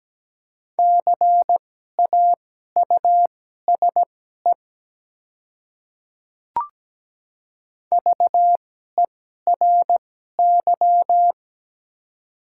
Morse code, heard as a keyed tone, about 17 wpm, spaced as for 12 wpm, 700 Hz. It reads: CAUSE VERY
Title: 2x Character Spacing - Top 300 Words - 17wpm